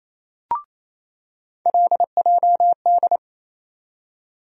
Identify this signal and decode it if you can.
Morse code, heard as a keyed tone, about 28 wpm, 700 Hz. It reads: LJD
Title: USA Call Sign Postfixes - 3 Letters 28wpm